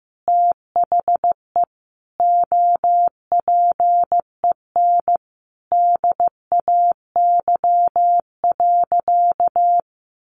Morse code, heard as a keyed tone, about 15 wpm, 700 Hz. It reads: THE OPEN DAY.
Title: Sentences from Top 1500 Words 15wpm